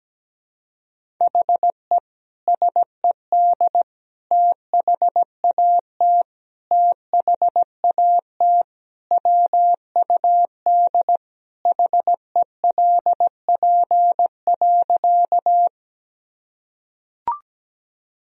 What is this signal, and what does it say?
Morse code, heard as a keyed tone, about 17 wpm, 700 Hz. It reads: HE SED THAT THAT WUD HELP.
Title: Sentences from Top 200 Words and Abbreviations 17wpm